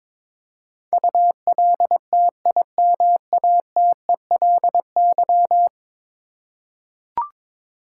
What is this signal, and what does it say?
Morse code, heard as a keyed tone, about 22 wpm, 700 Hz. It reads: ULTIMATELY
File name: practice / 22wpm / 10 Letter Words - First 3 Letters Given - 22wpm